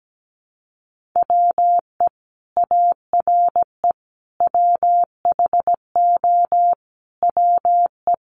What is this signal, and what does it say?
Morse code, heard as a keyed tone, about 17 wpm, 700 Hz. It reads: WE ARE WHO WE
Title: Sentences from Top 200 Words 17wpm